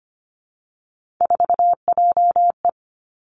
Morse code, heard as a keyed tone, about 25 wpm, 700 Hz. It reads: 4JE
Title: Three Challenging Letter-Number Combinations 25wpm